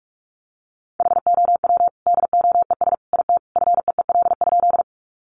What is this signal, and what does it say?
Morse code, heard as a keyed tone, about 44 wpm, 700 Hz. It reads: HOW DOES IT FEEL?
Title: Sentences from Top 400 Words 45wpm